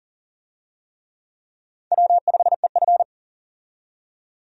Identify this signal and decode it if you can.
Morse code, heard as a keyed tone, about 37 wpm, 700 Hz. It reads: W5EF
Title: USA Call Signs - 4 Characters - 40wpm